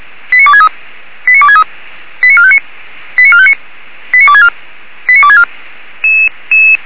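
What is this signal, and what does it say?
Five-tone selective call (ZVEI), heard as quick runs of five tones, tones of about 70 ms. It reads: 89263 89263 89459 89459 89264 89264
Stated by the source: ZVEI five-tone call 89273 received on SDR.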